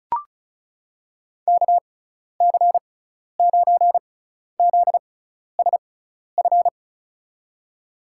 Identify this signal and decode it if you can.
Morse code, heard as a keyed tone, about 34 wpm, spaced as for 12 wpm, 700 Hz. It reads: KC9ZSF
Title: USA Call Signs - 6x Character Spacing - 35wpm